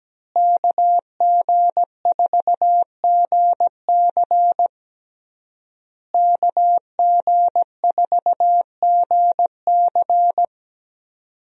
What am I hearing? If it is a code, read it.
Morse code, heard as a keyed tone, about 17 wpm, 700 Hz. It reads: KG4GC KG4GC
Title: USA Call Signs - 5 Characters - Repeated 3x - 17wpm